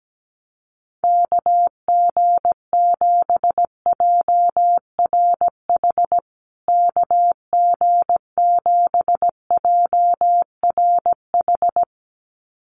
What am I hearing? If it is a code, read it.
Morse code, heard as a keyed tone, about 17 wpm, 700 Hz. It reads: KG7JRH KG7JRH
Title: USA Call Signs Repeated 2x 17wpm